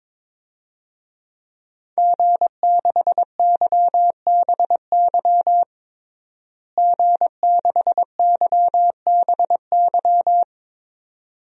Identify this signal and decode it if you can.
Morse code, heard as a keyed tone, about 22 wpm, 700 Hz. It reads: G6YBY G6YBY
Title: DXCC-100 Call Signs - 5 Characters - Repeated 2x - 22wpm